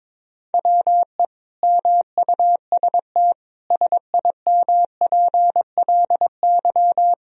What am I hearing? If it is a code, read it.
Morse code, heard as a keyed tone, about 22 wpm, 700 Hz. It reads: WE MUST SIMPLY
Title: Sentences from Top 1000 Words - Review Part 4 - 22wpm